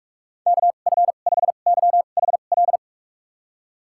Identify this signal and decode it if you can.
Morse code, heard as a keyed tone, about 46 wpm, spaced as for 32 wpm, 700 Hz. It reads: KF5YHL
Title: USA Call Signs - 6 Characters - 2x Character Spacing - 45wpm